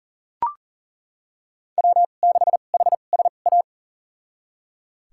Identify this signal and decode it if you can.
Morse code, heard as a keyed tone, about 37 wpm, spaced as for 28 wpm, 700 Hz. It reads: W6HSA
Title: USA Call Signs - 5 Characters - 2x Character Spacing - 40wpm